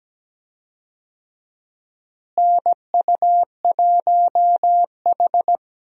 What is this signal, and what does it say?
Morse code, heard as a keyed tone, about 17 wpm, 700 Hz. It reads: NU1H
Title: USA Call Signs - 4 Characters - Repeated 3x - 17wpm